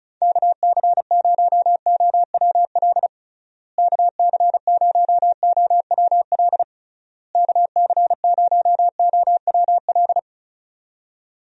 Morse code, heard as a keyed tone, about 34 wpm, 700 Hz. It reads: KC0OWL KC0OWL KC0OWL
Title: USA Call Signs - 6 Characters - Repeated 3x - 35wpm